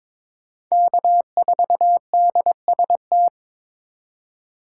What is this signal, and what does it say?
Morse code, heard as a keyed tone, about 26 wpm, 700 Hz. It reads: K4DST
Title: Speed Racing - USA Call Signs 22wpm